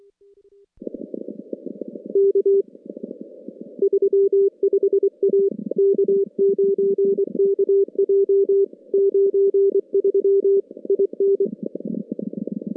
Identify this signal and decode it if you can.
Morse code, heard as a keyed tone, about 23 wpm, 400 Hz. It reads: K 35 A K9KJ 93 IN
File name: contests / CW-SS-k5tr- -2200